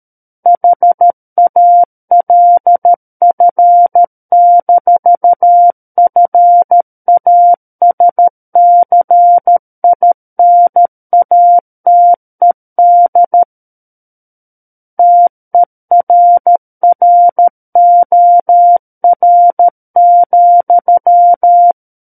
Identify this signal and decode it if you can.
Morse code, heard as a keyed tone, about 13 wpm, 700 Hz. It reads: HALF-FASCINATED TERROR,
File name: Book One / 04-Chapter 4 - 13 WPM